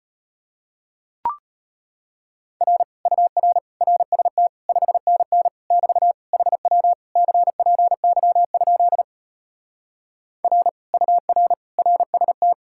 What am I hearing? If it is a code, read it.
Morse code, heard as a keyed tone, about 33 wpm, 700 Hz. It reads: R UR RST 5NN <BT> HW CPY? R UR RST